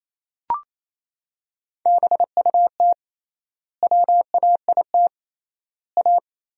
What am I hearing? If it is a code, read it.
Morse code, heard as a keyed tone, about 28 wpm, 700 Hz. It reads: BUT WAIT A
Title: 3x Word-Spacing - Sentences from Top 500 Words - 28wpm